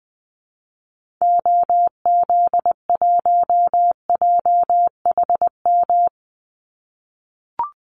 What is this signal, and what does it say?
Morse code, heard as a keyed tone, about 20 wpm, 700 Hz. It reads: OZ1JHM
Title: Call Signs - 20wpm